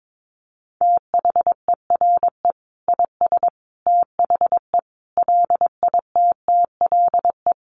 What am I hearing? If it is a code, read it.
Morse code, heard as a keyed tone, about 22 wpm, 700 Hz. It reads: THERE IS THE LITTLE
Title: Sentences from Top 200 Words 22wpm